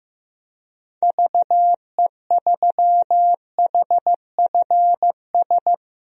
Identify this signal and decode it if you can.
Morse code, heard as a keyed tone, about 15 wpm, 700 Hz. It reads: VE3HFS
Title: DXCC-100 Call Signs - 6 Characters - Repeated 3x - 15wpm